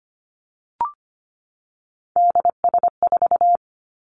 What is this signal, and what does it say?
Morse code, heard as a keyed tone, about 25 wpm, 700 Hz. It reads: DS4
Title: Three Challenging Letter-Number Combinations 25wpm